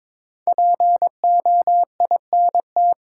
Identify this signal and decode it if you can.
Morse code, heard as a keyed tone, about 22 wpm, 700 Hz. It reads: POINT
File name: 5 Letter Words - First 3 Letters Given - 22wpm